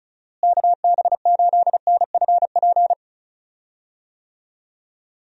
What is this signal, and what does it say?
Morse code, heard as a keyed tone, about 34 wpm, 700 Hz. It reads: KB8NFP
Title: USA Call Signs - 6 Characters - Repeated 2x - 35wpm